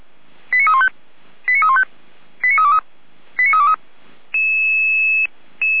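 Five-tone selective call (ZVEI), heard as quick runs of five tones, tones of about 70 ms. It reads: 89315 89315 89323 89323
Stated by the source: ZVEI five-tone call received by SDR.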